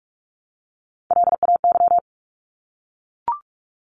Morse code, heard as a keyed tone, about 44 wpm, 700 Hz. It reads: LAY
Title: Top 500 Words 45wpm